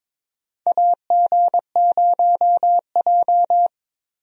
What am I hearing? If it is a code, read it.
Morse code, heard as a keyed tone, about 22 wpm, 700 Hz. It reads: AG0J
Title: USA Call Signs - 4 Characters - Repeated 2x - 22wpm